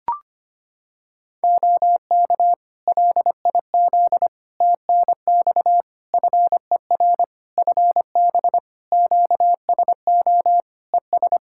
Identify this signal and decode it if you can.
Morse code, heard as a keyed tone, about 25 wpm, 700 Hz. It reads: OK LIZ TNX FER FB QSO ES
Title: Practice QSO - Exchange 4 - Ending - 25wpm